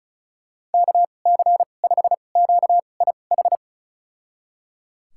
Morse code, heard as a keyed tone, about 36 wpm, spaced as for 25 wpm, 700 Hz. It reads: KC5QIH
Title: USA Call Signs - 6 Characters - 2x Character Spacing - 35wpm